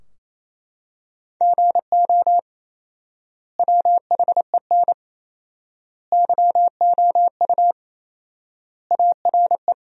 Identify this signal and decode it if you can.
Morse code, heard as a keyed tone, about 28 wpm, 700 Hz. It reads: GO WHEN YOU ARE
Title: 4x Word-Spacing - Sentences from Top 500 Words - 28wpm